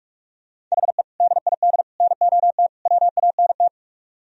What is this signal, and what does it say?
Morse code, heard as a keyed tone, about 44 wpm, 700 Hz. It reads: HE DID NOT WANT